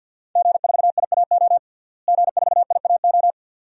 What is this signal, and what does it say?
Morse code, heard as a keyed tone, about 45 wpm, 700 Hz. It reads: G4IAO G4IAO
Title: DXCC-100 Call Signs - 5 Characters - Repeated 2x - 50wpm